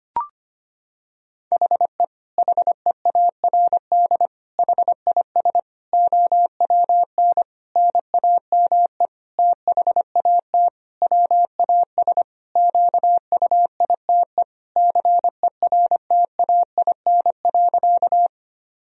Morse code, heard as a keyed tone, about 25 wpm, 700 Hz. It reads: HE HEARD HIS OWN NAME THAT WAS QUITE CERTAIN.
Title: Sentences from Top 1000 Words - Review Part 2 - 25wpm